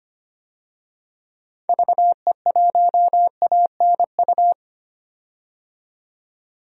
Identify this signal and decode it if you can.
Morse code, heard as a keyed tone, about 25 wpm, 700 Hz. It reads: VE1ANU
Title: DXCC-100 Call Signs - 6 Characters - 25wpm